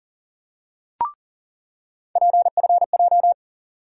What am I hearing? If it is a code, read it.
Morse code, heard as a keyed tone, about 40 wpm, 700 Hz. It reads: PFJ
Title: USA Call Sign Postfixes - 3 Letters 40wpm